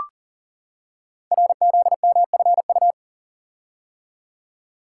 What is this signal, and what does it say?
Morse code, heard as a keyed tone, about 40 wpm, 690 Hz. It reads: RZMFU
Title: Code Groups - 5 Letter:Numbers - 40wpm